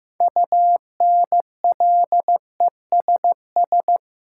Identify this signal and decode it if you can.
Morse code, heard as a keyed tone, about 15 wpm, 700 Hz. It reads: UNLESS
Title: Sentences from Top 1000 Words - Review Part 5 - New Word Drill - 15wpm